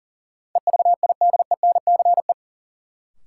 Morse code, heard as a keyed tone, about 40 wpm, 700 Hz. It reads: EVIDENCE
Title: Sentences from Top 1300 Words - New Word Drill Alt - 40wpm